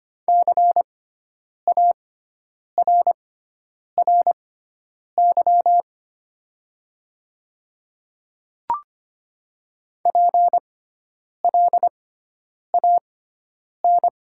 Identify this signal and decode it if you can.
Morse code, heard as a keyed tone, about 25 wpm, spaced as for 9 wpm, 690 Hz. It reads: CARRY PLAN
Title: 6x Character Spacing - Top 500 Words - 25wpm